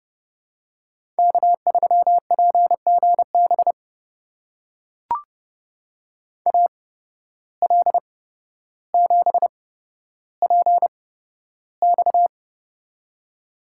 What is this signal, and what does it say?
Morse code, heard as a keyed tone, about 31 wpm, spaced as for 13 wpm, 700 Hz. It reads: K3PGB AL7PX
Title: USA Call Signs - 5 Characters - 8x Character Spacing - 30wpm